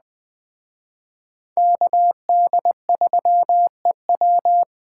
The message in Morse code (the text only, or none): KD3EW